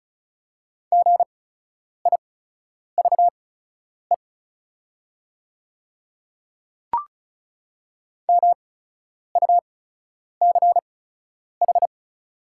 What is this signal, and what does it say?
Morse code, heard as a keyed tone, about 34 wpm, spaced as for 10 wpm, 700 Hz. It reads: GIVE MUCH
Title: 8x Character Spacing - Top 300 Words - 35wpm